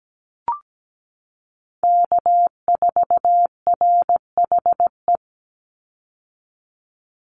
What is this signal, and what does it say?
Morse code, heard as a keyed tone, about 17 wpm, 700 Hz. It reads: K4RHE
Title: USA Call Signs 17wpm